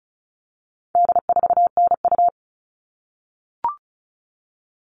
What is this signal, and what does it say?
Morse code, heard as a keyed tone, about 34 wpm, 700 Hz. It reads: D4NU
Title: All Cut Numbers - 4-Digit 35wpm